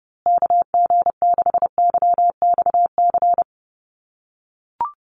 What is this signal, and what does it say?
Morse code, heard as a keyed tone, about 28 wpm, 700 Hz. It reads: KG6YXC
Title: USA Call Signs Repeated 2x 30wpm